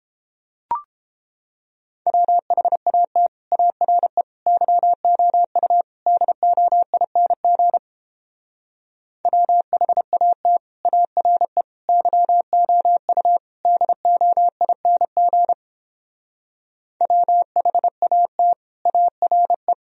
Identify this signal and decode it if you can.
Morse code, heard as a keyed tone, about 31 wpm, 700 Hz. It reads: WHAT ARE YOU DOING WHAT ARE YOU DOING WHAT ARE